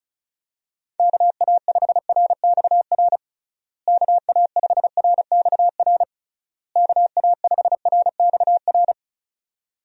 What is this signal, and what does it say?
Morse code, heard as a keyed tone, about 36 wpm, 700 Hz. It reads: KA5RXR KA5RXR KA5RXR
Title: USA Call Signs - 6 Characters - Repeated 3x - 35wpm